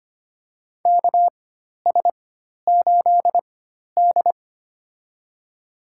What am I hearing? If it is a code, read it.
Morse code, heard as a keyed tone, about 25 wpm, spaced as for 12 wpm, 700 Hz. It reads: KS8D
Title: USA Call Signs - 4 Characters - 4x Character Spacing - 25wpm